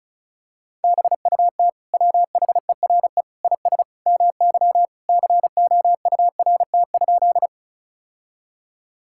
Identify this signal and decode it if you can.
Morse code, heard as a keyed tone, about 36 wpm, 700 Hz. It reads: BUT WHERE IS MY COURT?
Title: Sentences from Top 1600 Words 35wpm